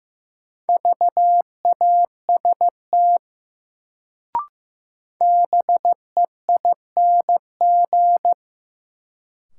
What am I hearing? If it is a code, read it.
Morse code, heard as a keyed tone, about 15 wpm, 700 Hz. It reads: VAST BEING
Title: Sets of 4 Words - New Word Drill - 15wpm